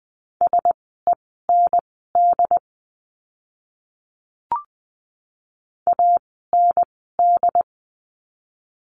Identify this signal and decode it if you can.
Morse code, heard as a keyed tone, about 20 wpm, spaced as for 15 wpm, 700 Hz. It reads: SEND AND
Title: 075 Lesson 13f - Letter L - Words 20wpm